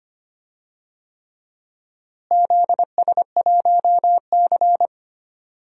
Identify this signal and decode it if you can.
Morse code, heard as a keyed tone, about 25 wpm, 700 Hz. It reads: ZS1C
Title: DXCC-100 Call Signs - 4 Characters - Repeated 3x - 25wpm